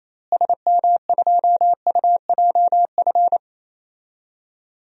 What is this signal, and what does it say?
Morse code, heard as a keyed tone, about 28 wpm, 700 Hz. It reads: SM2UJF
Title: DXCC-100 Call Signs - 6 Characters - Repeated 2x - 28wpm